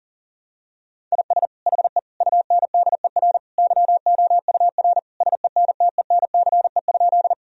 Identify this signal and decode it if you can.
Morse code, heard as a keyed tone, about 40 wpm, 700 Hz. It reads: IS HE UNDER YOUR SENTENCE?
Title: Sentences from Top 1000 Words - Review Part 1 - 40wpm